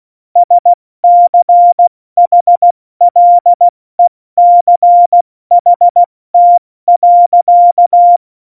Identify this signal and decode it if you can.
Morse code, heard as a keyed tone, about 16 wpm, 700 Hz. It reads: SCHLECHT.
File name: Morsekurs / Witze / Witz 20. Februar 2018, 70 F80